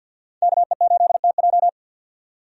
Morse code, heard as a keyed tone, about 52 wpm, 700 Hz. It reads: KE8TJ
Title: USA Call Signs - 5 Characters - 50wpm